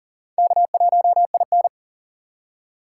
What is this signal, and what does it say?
Morse code, heard as a keyed tone, about 52 wpm, 700 Hz. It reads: K1IN